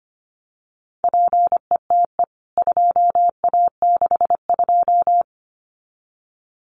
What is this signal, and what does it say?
Morse code, heard as a keyed tone, about 25 wpm, 700 Hz. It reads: PETE 2A62
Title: CWT Exchange 25wpm